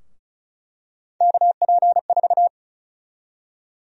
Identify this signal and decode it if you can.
Morse code, heard as a keyed tone, about 36 wpm, 700 Hz. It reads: KP4
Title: USA Call Sign Prefixes - Repeated 2x - 35wpm